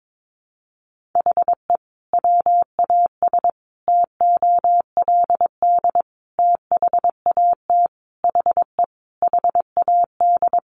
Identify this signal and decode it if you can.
Morse code, heard as a keyed tone, about 22 wpm, 700 Hz. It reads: HE WAS TOLD THAT HE HAD
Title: Sentences from Top 1500 Words 22wpm